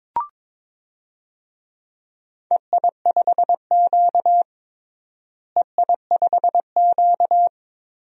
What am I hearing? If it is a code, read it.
Morse code, heard as a keyed tone, about 22 wpm, 700 Hz. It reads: EI5Q EI5Q
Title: DXCC-100 Call Signs - 4 Characters - Repeated 3x - 22wpm